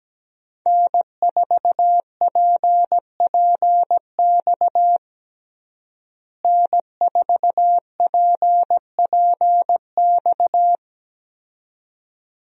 Morse code, heard as a keyed tone, about 17 wpm, 700 Hz. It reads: N4PPX N4PPX
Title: USA Call Signs - 5 Characters - Repeated 2x - 17wpm